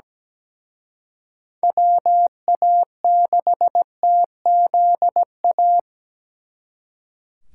Morse code, heard as a keyed tone, about 20 wpm, 700 Hz. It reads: WA6TZA
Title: Speed Racing - USA Call Signs 17wpm